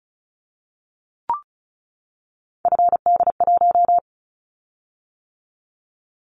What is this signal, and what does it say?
Morse code, heard as a keyed tone, about 36 wpm, 690 Hz. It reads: FD1
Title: Callsign Prefixes for Top 50 DXCC - 35wpm